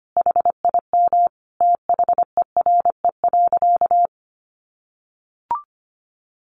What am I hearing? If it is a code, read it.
Morse code, heard as a keyed tone, about 25 wpm, 700 Hz. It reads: HIM THERE.
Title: Sentences from Top 1100 Words 25wpm